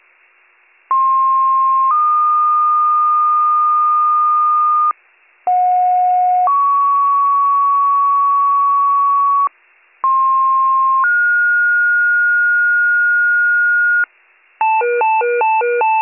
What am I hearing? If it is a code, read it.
Two-tone page, heard as a one-second tone, then a three-second tone, about 1060 then 1220 Hz, about 730 then 1120 Hz, about 1060 then 1430 Hz, with noise. It is heard from 0.9 to 14.0 s.